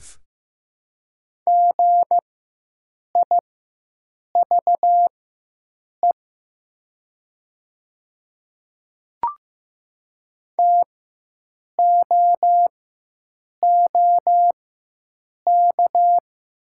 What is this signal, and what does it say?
Morse code, heard as a keyed tone, about 15 wpm, spaced as for 7 wpm, 700 Hz. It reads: GIVE TOOK
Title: 4x Character Spacing - Top 500 Words - 15wpm